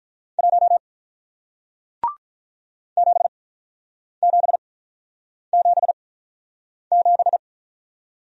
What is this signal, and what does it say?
Morse code, heard as a keyed tone, about 49 wpm, spaced as for 9 wpm, 700 Hz. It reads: 1 7777